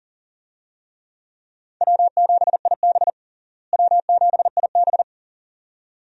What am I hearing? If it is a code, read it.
Morse code, heard as a keyed tone, about 40 wpm, 700 Hz. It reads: W7IB W7IB
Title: USA Call Signs - 4 Characters - Repeated 2x - 40wpm